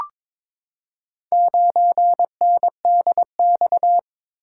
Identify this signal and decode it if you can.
Morse code, heard as a keyed tone, about 22 wpm, 700 Hz. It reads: 9NDX
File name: practice / 22wpm / Four Letter-Number Combinations - 22wpm